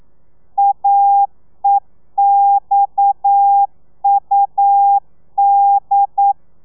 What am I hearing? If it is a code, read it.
Morse code, heard as a keyed tone, about 9 wpm, 800 Hz. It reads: AEXUD
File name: telegrafia / i2tfj / Lez7 40